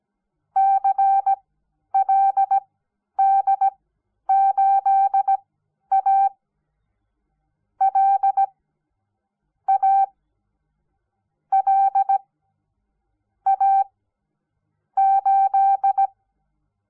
Morse code, heard as a keyed tone, about 17 wpm, spaced as for 10 wpm, 780 Hz. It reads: CLD8A L A L A 8
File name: Onlinelehrgang / morsen / DJ4UF Morsen06